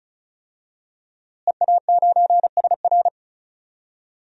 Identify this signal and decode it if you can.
Morse code, heard as a keyed tone, about 36 wpm, 700 Hz. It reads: EA9SR
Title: DXCC-100 Call Signs - 5 Characters - Repeated 2x - 35wpm